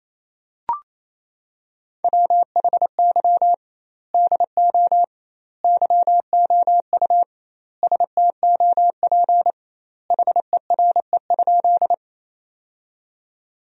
Morse code, heard as a keyed tone, about 27 wpm, 700 Hz. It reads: WHY DO YOU STOP HERE?
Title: 2x Word-Spacing - Sentences from Top 300 Words - 28wpm